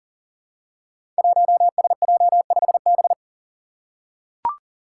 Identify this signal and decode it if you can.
Morse code, heard as a keyed tone, about 41 wpm, 700 Hz. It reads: 1SJ5B
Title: Five Challenging Letter-Number Combinations 40wpm